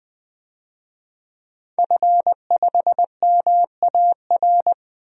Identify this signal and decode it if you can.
Morse code, heard as a keyed tone, about 21 wpm, 700 Hz. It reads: F5MAR
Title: DXCC-100 Call Signs - 5 Characters - Repeated 2x - 20wpm